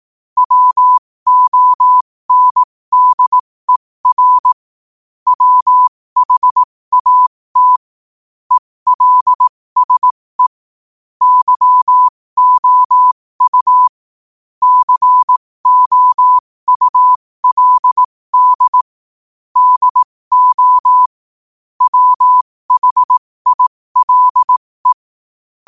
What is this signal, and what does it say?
Morse code, heard as a keyed tone, about 18 wpm, spaced as for 15 wpm, 1000 Hz. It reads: WONDER WHAT ELSE YOU COULD DO WHILE
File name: qotdfiles / 15wpm